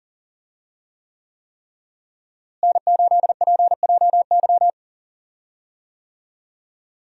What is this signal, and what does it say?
Morse code, heard as a keyed tone, about 40 wpm, 700 Hz. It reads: N8PJY
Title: USA Call Signs - 5 Characters - Repeated 3x - 40wpm